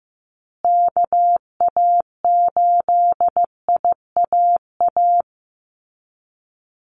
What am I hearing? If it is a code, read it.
Morse code, heard as a keyed tone, about 15 wpm, 700 Hz. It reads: KA8IAA
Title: USA Call Signs Repeated 3x 15wpm